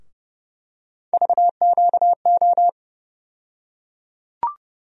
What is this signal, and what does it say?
Morse code, heard as a keyed tone, about 30 wpm, 700 Hz. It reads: VQO
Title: USA Call Sign Postfixes - 3 Letters 30wpm